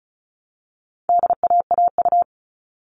Morse code, heard as a keyed tone, about 34 wpm, 700 Hz. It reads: DAAU
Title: All Cut Numbers - 4-Digit 35wpm